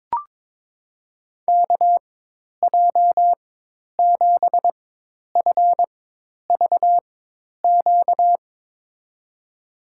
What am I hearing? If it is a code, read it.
Morse code, heard as a keyed tone, about 22 wpm, spaced as for 10 wpm, 700 Hz. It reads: KJ7FVQ
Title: USA Call Signs - 6 Characters - 4x Character Spacing - 22wpm